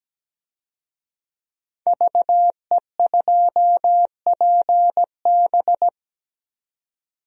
Morse code, heard as a keyed tone, about 17 wpm, 700 Hz. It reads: VE2PB
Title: DXCC-100 Call Signs - 5 Characters - 17wpm